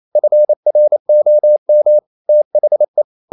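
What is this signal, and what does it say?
Morse code, heard as a keyed tone, about 28 wpm, 600 Hz. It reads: FROM THE
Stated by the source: Morse code practice at 28 wpm, text of Ken Follett's novel Never in English, from CW med Gnister.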